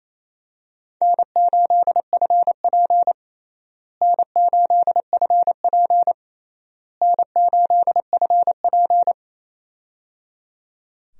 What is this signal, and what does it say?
Morse code, heard as a keyed tone, about 28 wpm, 700 Hz. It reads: N8FP N8FP N8FP
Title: USA Call Signs - 4 Characters - Repeated 3x - 28wpm